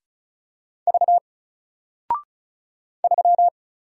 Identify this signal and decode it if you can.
Morse code, heard as a keyed tone, about 34 wpm, 700 Hz. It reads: V 3
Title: Single Letter-Number Alt 35wpm